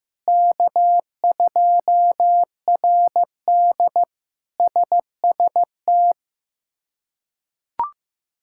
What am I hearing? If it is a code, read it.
Morse code, heard as a keyed tone, about 15 wpm, 700 Hz. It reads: K2RD SST
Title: K1USN SST QSO Practice - 15wpm